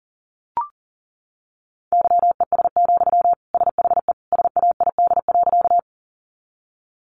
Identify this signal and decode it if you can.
Morse code, heard as a keyed tone, about 40 wpm, 700 Hz. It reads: YES, SHE SAID.
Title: Sentences from Top 500 Words 40wpm